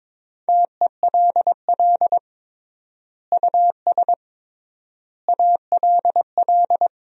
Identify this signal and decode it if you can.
Morse code, heard as a keyed tone, about 22 wpm, 700 Hz. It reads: TELL US ALL
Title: 3x Word-Spacing - Sentences from Top 200 Words - 22wpm